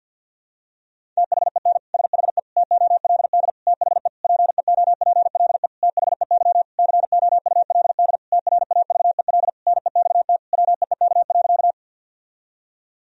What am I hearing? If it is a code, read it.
Morse code, heard as a keyed tone, about 49 wpm, 700 Hz. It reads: THEN SHE TOLD THE PEOPLE THEY COULD TRAVEL NEXT WEEK.
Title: Sentences from Top 1000 Words - Review Part 2 - 50wpm